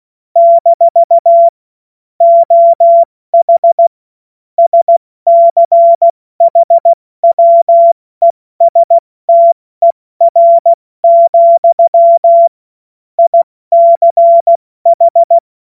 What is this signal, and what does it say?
Morse code, heard as a keyed tone, about 16 wpm, 680 Hz. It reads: - OH SCHWESTER, ICH
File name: Morsekurs / Witze / Witz 27. Dezember 2022 70 F80